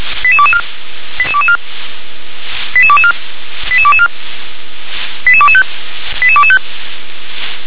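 Five-tone selective call (ZVEI), heard as quick runs of five tones, tones of about 70 ms. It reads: 88224 88224 88224 88224 88225 88225
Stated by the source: ZVEI five-tone selective call 88222 received by SDR.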